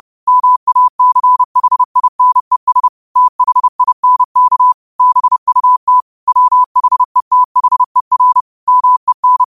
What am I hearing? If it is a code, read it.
Morse code, heard as a keyed tone, about 30 wpm, 1000 Hz. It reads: MACHINES THINK BUT WHETHER MEN